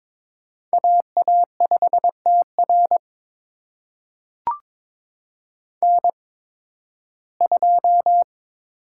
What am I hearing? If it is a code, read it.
Morse code, heard as a keyed tone, about 22 wpm, spaced as for 14 wpm, 700 Hz. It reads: AA5TR N2
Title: USA Call Signs - 5 Characters - 8x Character Spacing - 22wpm